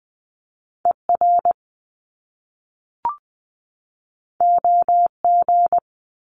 Morse code, heard as a keyed tone, about 20 wpm, 700 Hz. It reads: ER OG
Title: Top Two Letter Combinations in Words 20wpm